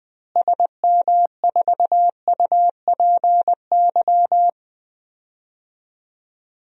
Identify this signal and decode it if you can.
Morse code, heard as a keyed tone, about 20 wpm, 700 Hz. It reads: SM4UPY
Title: DXCC-100 Call Signs - 6 Characters - Repeated 2x - 20wpm